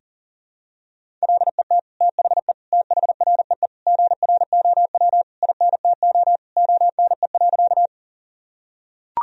Morse code, heard as a keyed tone, about 40 wpm, 700 Hz. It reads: LET THE THREE GROW INTO ONE.
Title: Sentences from Top 1000 Words - Review Part 1 - 40wpm